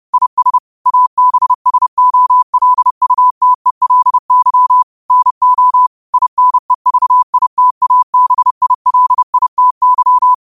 Morse code, heard as a keyed tone, about 30 wpm, 1000 Hz. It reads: IS ABSOLUTELY NO INEVITABILITY
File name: qotdfiles / 30wpm